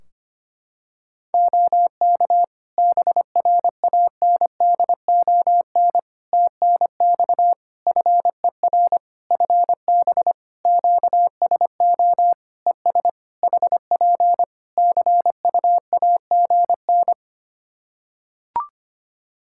Morse code, heard as a keyed tone, about 25 wpm, 700 Hz. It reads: OK BRANDON TNX FER FB QSO ES HP CUAGN
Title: K1USN SST QSO Practice 25wpm